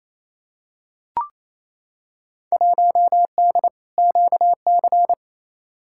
Morse code, heard as a keyed tone, about 28 wpm, 700 Hz. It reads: 1D QC
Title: Field Day Exchanges Encore - 28wpm